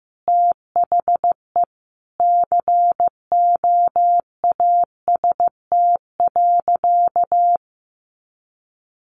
Morse code, heard as a keyed tone, about 15 wpm, 700 Hz. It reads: THE COAST.
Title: Sentences from Top 1300 Words 15wpm